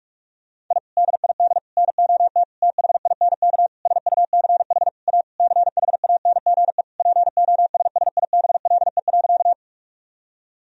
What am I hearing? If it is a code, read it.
Morse code, heard as a keyed tone, about 46 wpm, 700 Hz. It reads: I DID NOT THINK SUCH A CHANGE POSSIBLE.
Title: Sentences from Top 1000 Words - Review Part 1 - 45wpm